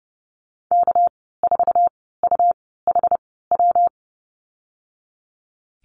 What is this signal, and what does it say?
Morse code, heard as a keyed tone, about 30 wpm, spaced as for 17 wpm, 700 Hz. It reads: K4UHW
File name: Lesson 37g - Letter x - Call Signs 30wpm